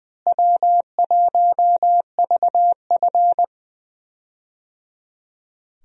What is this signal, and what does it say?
Morse code, heard as a keyed tone, about 21 wpm, 700 Hz. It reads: W1VF
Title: USA Call Signs - 4 Characters - Repeated 3x - 20wpm